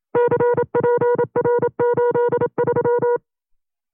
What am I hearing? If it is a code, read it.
Morse code, heard as a keyed tone, about 27 wpm, 480 Hz. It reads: CPR83
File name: CW-practice/25WPM